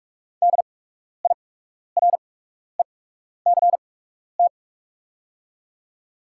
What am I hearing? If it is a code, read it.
Morse code, heard as a keyed tone, about 44 wpm, spaced as for 12 wpm, 690 Hz. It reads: DIRECT
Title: 8x Character Spacing - Top 500 Words - 45wpm